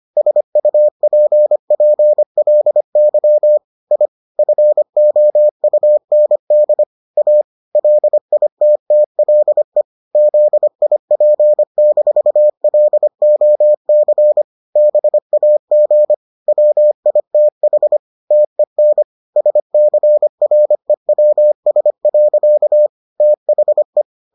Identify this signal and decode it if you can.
Morse code, heard as a keyed tone, about 25 wpm, 600 Hz. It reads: SUPPLY I FOUND A LITTLE ZIP-LOC BAG WITH TEN SCREWS. THE